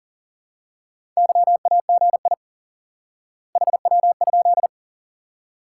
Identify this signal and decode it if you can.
Morse code, heard as a keyed tone, about 40 wpm, 700 Hz. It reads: YAGI HW?
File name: IQE Course 3 - Lesson 89 - YAGI - Two Elements - 40wpm